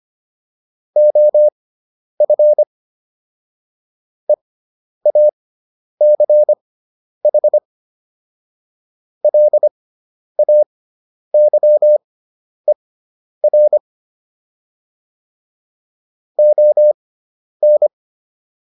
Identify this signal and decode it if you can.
Morse code, heard as a keyed tone, about 25 wpm, spaced as for 10 wpm, 600 Hz. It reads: OF EACH LAYER ON